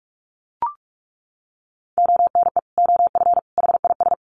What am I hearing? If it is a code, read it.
Morse code, heard as a keyed tone, about 44 wpm, 700 Hz. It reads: ONE OF HIS
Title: Sets of 3 Words 45wpm